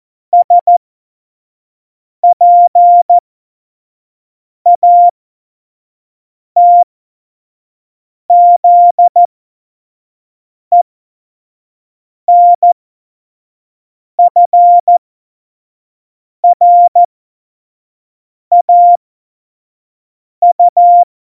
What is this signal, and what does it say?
Morse code, heard as a keyed tone, about 14 wpm, spaced as for 5 wpm, 700 Hz. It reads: SPATZENFRAU
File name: Morsekurs / Witze / Witz 06. November 2018, 25 F70